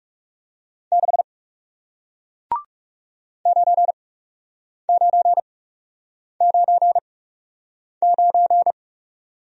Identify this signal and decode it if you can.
Morse code, heard as a keyed tone, about 39 wpm, spaced as for 8 wpm, 700 Hz. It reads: 6 9999